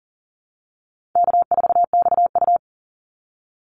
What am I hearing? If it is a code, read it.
Morse code, heard as a keyed tone, about 38 wpm, 700 Hz. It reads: K4XU
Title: Easy Call Signs - 40wpm